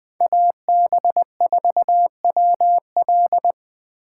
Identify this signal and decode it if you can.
Morse code, heard as a keyed tone, about 21 wpm, 700 Hz. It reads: AB4WL
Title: 2023 SOTA Chaser Call Signs - 20wpm